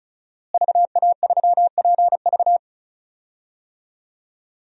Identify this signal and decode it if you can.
Morse code, heard as a keyed tone, about 34 wpm, 700 Hz. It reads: VA3PV
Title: DXCC-100 Call Signs - 5 Characters - Repeated 3x - 35wpm